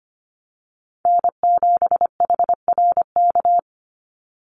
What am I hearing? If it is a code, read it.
Morse code, heard as a keyed tone, about 25 wpm, 700 Hz. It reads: N7HRK
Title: Call Signs - 25wpm